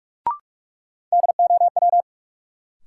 Morse code, heard as a keyed tone, about 46 wpm, 700 Hz. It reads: DOW
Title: Sets of 5 Words - New Word Drill - 45wpm